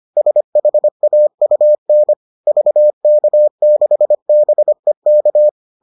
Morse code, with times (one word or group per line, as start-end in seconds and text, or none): 0.2-2.1 SHAUN
2.5-5.5 VK6BEK